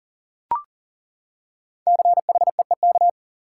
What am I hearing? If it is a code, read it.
Morse code, heard as a keyed tone, about 37 wpm, 700 Hz. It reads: CHEEK
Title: Five Letter Words - 40wpm